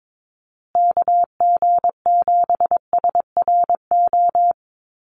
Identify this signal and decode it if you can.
Morse code, heard as a keyed tone, about 22 wpm, 700 Hz. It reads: KG7SRO
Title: USA Call Signs Repeated 3x 22wpm